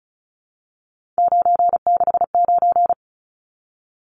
Morse code, heard as a keyed tone, about 34 wpm, 700 Hz. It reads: 969
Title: Numbers - 3-Digit 35wpm